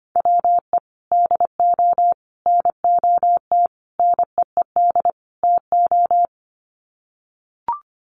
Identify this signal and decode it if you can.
Morse code, heard as a keyed tone, about 25 wpm, 700 Hz. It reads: WE DO NOT NEED TO
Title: Sets of 5 Words Encore 25wpm